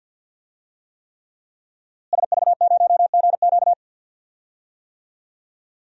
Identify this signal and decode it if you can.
Morse code, heard as a keyed tone, about 51 wpm, 700 Hz. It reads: SV0GQ
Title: DXCC-100 Call Signs - 5 Characters - Repeated 3x - 50wpm